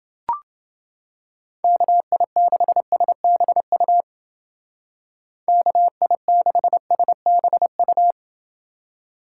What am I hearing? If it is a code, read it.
Morse code, heard as a keyed tone, about 29 wpm, 700 Hz. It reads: KI6SBU KI6SBU